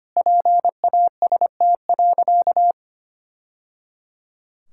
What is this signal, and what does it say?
Morse code, heard as a keyed tone, about 25 wpm, 700 Hz. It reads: PAST.
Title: Sentences from Top 1000 Words - Review Part 6 - 25wpm